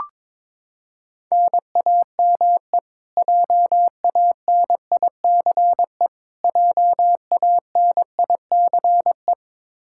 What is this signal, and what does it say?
Morse code, heard as a keyed tone, about 22 wpm, 700 Hz. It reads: NAME JANICE JANICE
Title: Practice QSO - Exchange 1 - Beginning - 22wpm